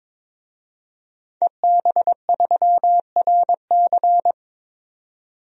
Morse code, heard as a keyed tone, about 22 wpm, 700 Hz. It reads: EB3RC
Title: DXCC-100 Call Signs - 5 Characters - 22wpm